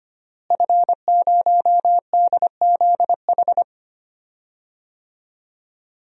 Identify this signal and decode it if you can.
Morse code, heard as a keyed tone, about 25 wpm, 700 Hz. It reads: F0DZH
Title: DXCC-100 Call Signs - 5 Characters - Repeated 2x - 25wpm